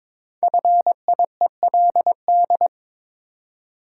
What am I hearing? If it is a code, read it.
Morse code, heard as a keyed tone, about 23 wpm, 700 Hz. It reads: FIELD
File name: 5 Letter Words - First 3 Letters Given - 22wpm